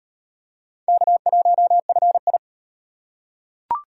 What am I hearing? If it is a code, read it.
Morse code, heard as a keyed tone, about 33 wpm, 700 Hz. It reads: K1FI